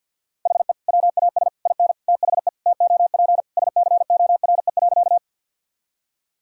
Morse code, heard as a keyed tone, about 50 wpm, 700 Hz. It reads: HE WAS IN THE TOP SCORE.
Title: Sentences from Top 1600 Words 50wpm